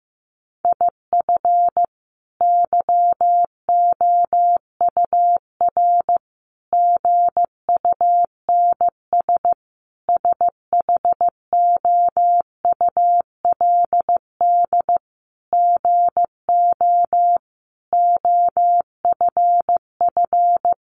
Morse code, heard as a keyed tone, about 15 wpm, 700 Hz. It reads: IF YOUR GUNS SHOULD GO OFF
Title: A Sound Of Thunder 15wpm